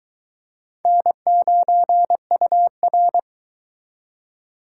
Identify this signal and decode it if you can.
Morse code, heard as a keyed tone, about 23 wpm, 700 Hz. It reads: N9UR